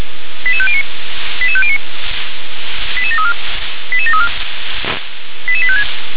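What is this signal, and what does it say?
Five-tone selective call (ZVEI), heard as quick runs of five tones, tones of about 70 ms. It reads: 88509 88509 88734 88734 88856
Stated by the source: ZVEI five-tone selective call received via SDR.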